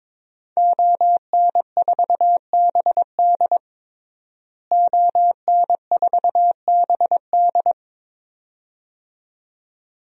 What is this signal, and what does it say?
Morse code, heard as a keyed tone, about 22 wpm, 700 Hz. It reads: ON4BD ON4BD
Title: DXCC-100 Call Signs - 5 Characters - Repeated 2x - 22wpm